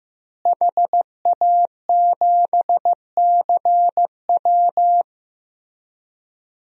Morse code, heard as a keyed tone, about 15 wpm, 700 Hz. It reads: HA7CW
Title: DXCC-100 Call Signs - 5 Characters - Repeated 3x - 15wpm